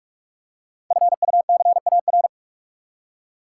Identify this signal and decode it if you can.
Morse code, heard as a keyed tone, about 44 wpm, 700 Hz. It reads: FUCAR
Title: Code Groups - 5 Letter:Numbers - 45wpm